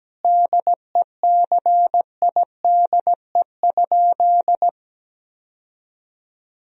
Morse code, heard as a keyed tone, about 17 wpm, 700 Hz. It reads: DECIDE?
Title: Sentences from Top 1000 Words - Review Part 3 - 17wpm